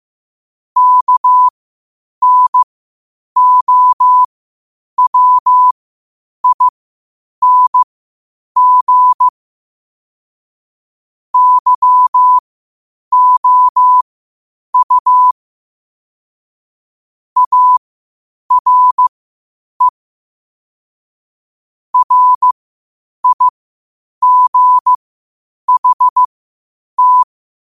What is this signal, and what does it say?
Morse code, heard as a keyed tone, about 15 wpm, spaced as for 8 wpm, 1000 Hz. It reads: KNOWING YOU ARE RIGHT
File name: qotdfiles / 8wpm